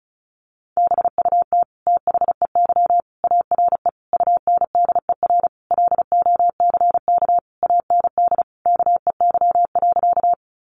Morse code, heard as a keyed tone, about 34 wpm, 700 Hz. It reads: BUT THEY ARE UNDER LOCK AND KEY.
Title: Sentences from Top 1300 Words 35wpm